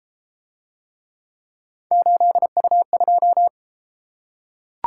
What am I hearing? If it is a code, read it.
Morse code, heard as a keyed tone, about 32 wpm, 700 Hz. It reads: 8U2